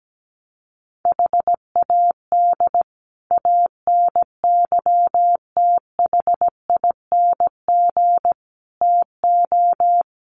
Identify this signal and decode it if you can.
Morse code, heard as a keyed tone, about 17 wpm, 700 Hz. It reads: HAD ANYTHING TO
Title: Sets of 5 Words Encore 17wpm